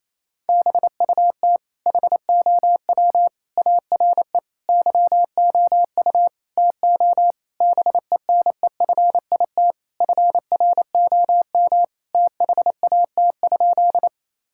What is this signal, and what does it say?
Morse code, heard as a keyed tone, about 28 wpm, 700 Hz. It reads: BUT HOW ARE YOU TO BENEFIT FROM THAT?
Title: Sentences from Top 1000 Words - Review Part 6 - 28wpm